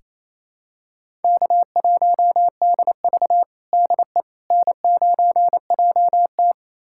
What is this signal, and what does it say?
Morse code, heard as a keyed tone, about 28 wpm, 700 Hz. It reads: K1DV DE N9JT
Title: USA Callsign DE Callsign - 4 Characters - 28wpm